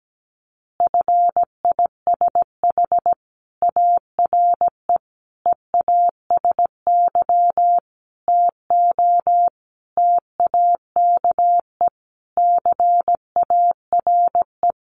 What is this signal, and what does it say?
Morse code, heard as a keyed tone, about 17 wpm, 700 Hz. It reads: FISH ARE EASY TO TAKE CARE
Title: Additional Easy Phrases 17wpm